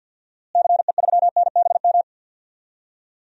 Morse code, heard as a keyed tone, about 50 wpm, 700 Hz. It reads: CE3NBM
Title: DXCC-100 Call Signs - 6 Characters - Repeated 3x - 50wpm